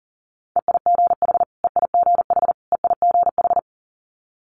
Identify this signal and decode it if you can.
Morse code, heard as a keyed tone, about 42 wpm, 700 Hz. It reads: EIGH EIGH EIGH
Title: Multi-letter Phonemes Repeated 3x 40wpm